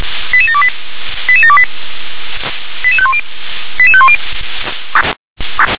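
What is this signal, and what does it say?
Five-tone selective call (ZVEI), heard as quick runs of five tones, tones of about 70 ms. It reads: 88728 88728 88510 88510
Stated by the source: ZVEI five-tone call 88730 received by SDR.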